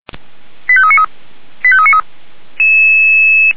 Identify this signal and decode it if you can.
Five-tone selective call (ZVEI), heard as quick runs of five tones, tones of about 70 ms. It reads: 85393 85393